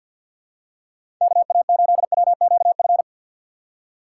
Morse code, heard as a keyed tone, about 53 wpm, 700 Hz. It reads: KA8WQF
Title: USA Call Signs - 6 Characters - 6x Character Spacing - 50wpm